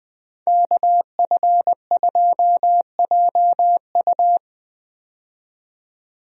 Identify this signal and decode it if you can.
Morse code, heard as a keyed tone, about 21 wpm, 700 Hz. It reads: KF2JU
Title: USA Call Signs - 5 Characters - 20wpm